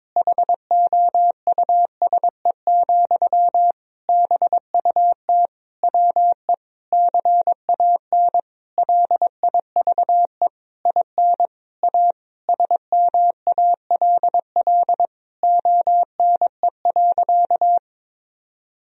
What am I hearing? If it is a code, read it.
Morse code, heard as a keyed tone, about 22 wpm, 700 Hz. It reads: HOUSE, BUT WE CAN LIVE IN A SMALL ONE.
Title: Sentences from Top 1000 Words - Review Part 1 - 22wpm